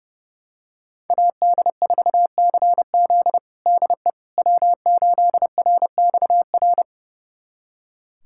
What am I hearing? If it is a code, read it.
Morse code, heard as a keyed tone, about 30 wpm, 700 Hz. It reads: AD4CZ DE W8RXR
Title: USA Callsign DE Callsign - 5 Characters - 30wpm